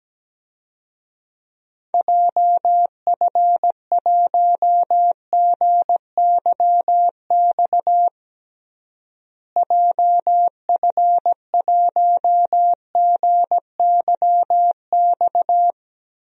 Morse code, heard as a keyed tone, about 17 wpm, 700 Hz. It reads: JF1GYX JF1GYX
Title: DXCC-100 Call Signs - 6 Characters - Repeated 3x - 17wpm